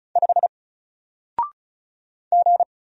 Morse code, heard as a keyed tone, about 36 wpm, 700 Hz. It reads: 5 G
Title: Single Letter-Number Alt 35wpm